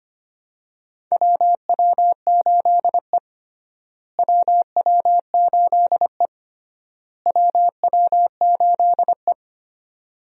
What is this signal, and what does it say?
Morse code, heard as a keyed tone, about 25 wpm, 700 Hz. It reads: WW8E WW8E WW8E
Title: USA Call Signs - 4 Characters - Repeated 3x - 25wpm